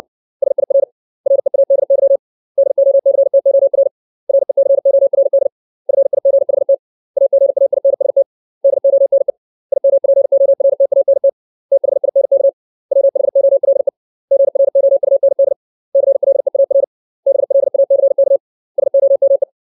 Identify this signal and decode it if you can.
Morse code, heard as a keyed tone, about 60 wpm, 550 Hz. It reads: SIR READY DOCTOR RECORD FIGHT AGAINST DONE IMPORTANT THINK WHOLE GROUND CLEAR BLACK SOME
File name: 632 words / 632 Words 60WPM